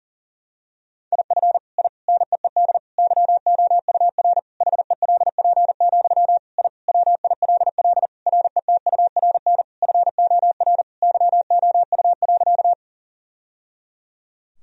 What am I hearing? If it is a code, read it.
Morse code, heard as a keyed tone, about 40 wpm, 700 Hz. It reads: IF I NEED YOUR HELP, I WILL RETURN FOR YOU.
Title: Sentences from Top 1000 Words - Review Part 1 - 40wpm